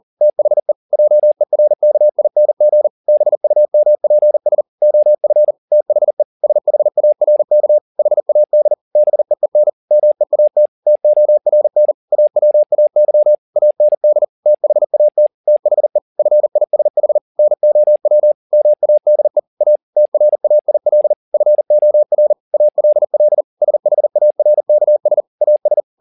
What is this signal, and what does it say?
Morse code, heard as a keyed tone, about 40 wpm, 600 Hz. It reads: THE JERKING BUMPS OF THE SHARK HAD BEEN MEAT TORN AWAY AND THAT THE FISH NOW MADE A TRAIL FOR ALL SHARKS AS
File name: the old man and the sea 2 40wpm